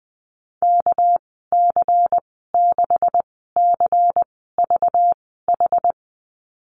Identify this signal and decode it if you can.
Morse code, heard as a keyed tone, about 20 wpm, spaced as for 15 wpm, 700 Hz. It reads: KC6CVH
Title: 207 Lesson 33g - Letter j - Call Signs 20wpm